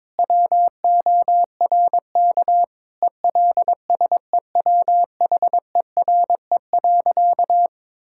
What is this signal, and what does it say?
Morse code, heard as a keyed tone, about 22 wpm, 700 Hz. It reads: WORK ELSEWHERE.
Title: Sentences from Top 1600 Words 22wpm